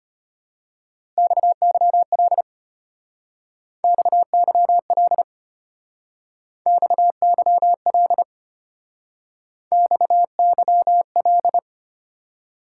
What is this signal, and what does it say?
Morse code, heard as a keyed tone, about 32 wpm, 700 Hz. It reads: XYL XYL XYL XYL